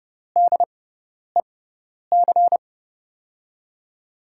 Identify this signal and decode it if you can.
Morse code, heard as a keyed tone, about 29 wpm, spaced as for 10 wpm, 700 Hz. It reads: DEC
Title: USA Call Signs - 5 Characters - 6x Character Spacing - 30wpm